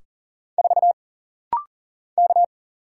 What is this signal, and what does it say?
Morse code, heard as a keyed tone, about 40 wpm, 700 Hz. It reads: 4 K
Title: Single Letter-Number - Rapid-Fire 40wpm